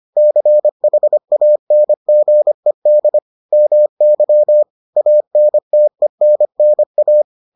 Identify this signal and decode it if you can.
Morse code, heard as a keyed tone, about 25 wpm, 600 Hz. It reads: CHANGED MY ANTENNA